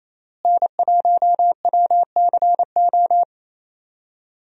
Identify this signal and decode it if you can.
Morse code, heard as a keyed tone, about 28 wpm, 700 Hz. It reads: N1WCO
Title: USA Call Signs - 5 Characters - Repeated 3x - 28wpm